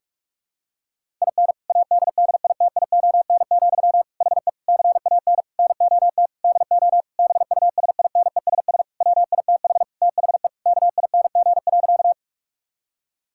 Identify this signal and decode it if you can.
Morse code, heard as a keyed tone, about 46 wpm, 700 Hz. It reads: IN ADDITION, HE CAN NOT DO BUSINESS WITH THE KING.
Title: Sentences from Top 1000 Words - Review Part 6 - 45wpm